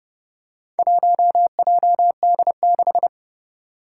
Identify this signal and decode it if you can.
Morse code, heard as a keyed tone, about 30 wpm, 690 Hz. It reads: 1JD6
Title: Four Challenging Letter-Number Combinations 30wpm